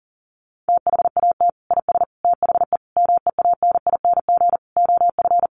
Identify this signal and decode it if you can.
Morse code, heard as a keyed tone, about 40 wpm, 700 Hz. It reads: THAT IS THE MEANING OF
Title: Sentences from Top 900 Words 40wpm